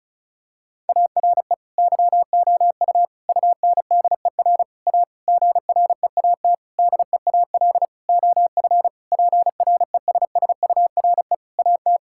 Morse code, heard as a keyed tone, about 36 wpm, 700 Hz. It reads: ARE YOU UNDER A GREAT DEAL OF PRESSURE AT
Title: Sentences from Top 1000 Words - Review Part 6 - 35wpm